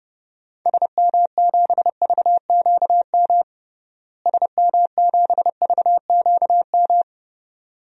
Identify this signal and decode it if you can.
Morse code, heard as a keyed tone, about 30 wpm, 700 Hz. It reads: SM7VQM SM7VQM
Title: DXCC-100 Call Signs - 6 Characters - Repeated 2x - 30wpm